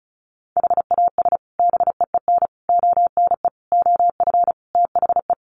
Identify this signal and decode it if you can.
Morse code, heard as a keyed tone, about 34 wpm, 700 Hz. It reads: HAS BEEN ONE OF THE
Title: Sets of 5 Words Encore 35wpm